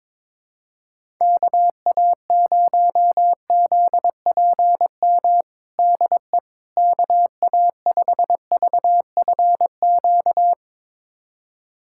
Morse code, heard as a keyed tone, about 22 wpm, 700 Hz. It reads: KA0ZPM DE KA5VFQ
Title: USA Callsign DE Callsign - 6 Characters - 22wpm